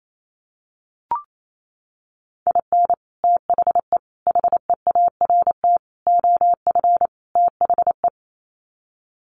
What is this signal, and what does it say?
Morse code, heard as a keyed tone, about 28 wpm, 700 Hz. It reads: IN THE HEART OF THE
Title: Sets of 5 Words 28wpm